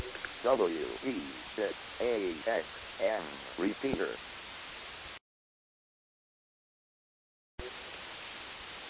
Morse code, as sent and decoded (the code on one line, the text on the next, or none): none